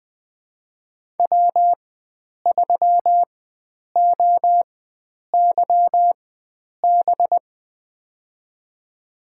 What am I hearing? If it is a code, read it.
Morse code, heard as a keyed tone, about 20 wpm, spaced as for 9 wpm, 700 Hz. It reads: W3OYB
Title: USA Call Signs - 5 Characters - 4x Character Spacing - 20wpm